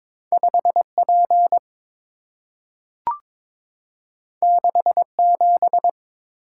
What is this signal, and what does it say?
Morse code, heard as a keyed tone, about 22 wpm, 690 Hz. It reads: 5P 67
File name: practice / 22wpm / Two Letter-Number Combinations - 22wpm